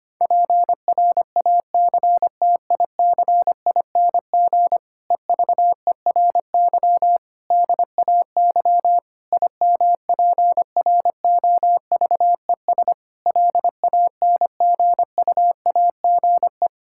Morse code, heard as a keyed tone, about 25 wpm, 700 Hz. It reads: PRACTICING EVERY DAY IMPROVES LANGUAGE
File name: practice / 25wpm / Difficult Sentences with Hints - 25wpm